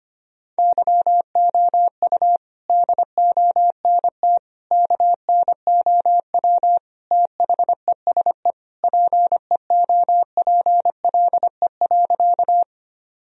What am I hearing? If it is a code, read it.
Morse code, heard as a keyed tone, about 25 wpm, 700 Hz. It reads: YOU DONT KNOW THESE PEOPLE.
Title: Sentences from Top 1000 Words - Review Part 1 - 25wpm